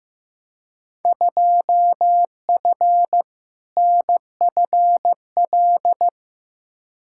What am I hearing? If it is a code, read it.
Morse code, heard as a keyed tone, about 15 wpm, 700 Hz. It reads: 2F NFL
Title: Field Day Exchanges - 15wpm